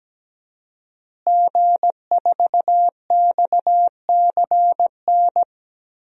Morse code, heard as a keyed tone, about 17 wpm, 700 Hz. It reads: G4XCN
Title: DXCC-100 Call Signs - 5 Characters - Repeated 2x - 17wpm